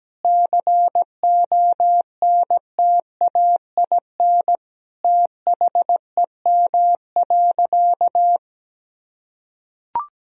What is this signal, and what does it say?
Morse code, heard as a keyed tone, about 17 wpm, 700 Hz. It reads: CONTAIN THEM.
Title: Sentences from Top 1000 Words - Review Part 2 - 17wpm